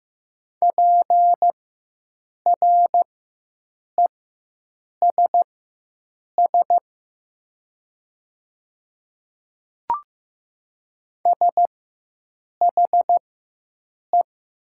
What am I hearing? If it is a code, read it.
Morse code, heard as a keyed tone, about 15 wpm, spaced as for 7 wpm, 700 Hz. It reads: PRESS SHE
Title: 4x Character Spacing - Top 500 Words - 15wpm